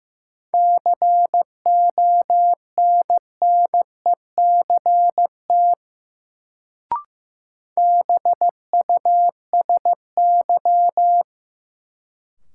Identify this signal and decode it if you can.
Morse code, heard as a keyed tone, about 15 wpm, 700 Hz. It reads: CONNECT BUSY
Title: Sentences from Top 1500 Words - New Word Drill Alt - 15wpm